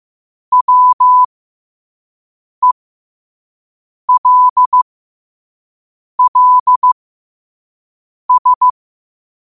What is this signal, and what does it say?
Morse code, heard as a keyed tone, about 15 wpm, spaced as for 5 wpm, 1000 Hz. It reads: WELLS
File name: qotdfiles / 5wpm